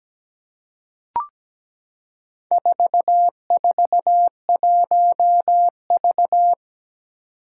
Morse code, heard as a keyed tone, about 17 wpm, 700 Hz. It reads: 441V
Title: Four Challenging Letter-Number Combinations 17wpm